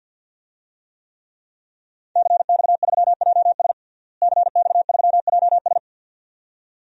Morse code, heard as a keyed tone, about 48 wpm, 700 Hz. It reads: CX3JS CX3JS
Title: DXCC-100 Call Signs - 5 Characters - Repeated 2x - 50wpm